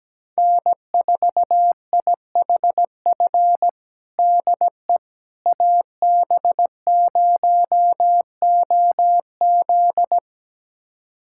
Morse code, heard as a keyed tone, about 17 wpm, 700 Hz. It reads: N4IHF DE AB0OZ
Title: USA Callsign DE Callsign - 5 Characters - 17wpm